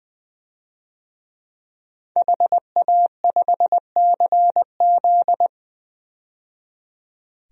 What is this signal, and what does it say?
Morse code, heard as a keyed tone, about 20 wpm, 700 Hz. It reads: HA5CZ
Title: DXCC-100 Call Signs - 5 Characters - 20wpm